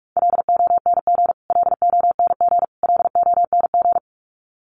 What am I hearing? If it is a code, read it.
Morse code, heard as a keyed tone, about 44 wpm, 700 Hz. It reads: LONG LONG LONG
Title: Top 500 Words Repeated 45wpm